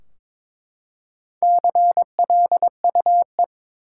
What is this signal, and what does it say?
Morse code, heard as a keyed tone, about 22 wpm, 700 Hz. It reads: CLUE
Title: Four Letter Words - 22wpm